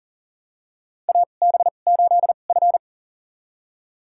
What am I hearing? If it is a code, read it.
Morse code, heard as a keyed tone, about 40 wpm, spaced as for 29 wpm, 700 Hz. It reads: AB8F
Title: USA Call Signs - 4 Characters - 2x Character Spacing - 40wpm